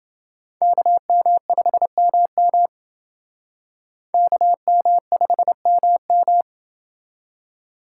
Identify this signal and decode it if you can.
Morse code, heard as a keyed tone, about 28 wpm, 700 Hz. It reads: KM5MM KM5MM